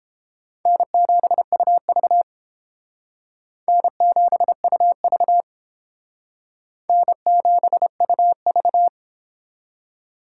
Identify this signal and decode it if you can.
Morse code, heard as a keyed tone, about 31 wpm, 700 Hz. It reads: N7UV N7UV N7UV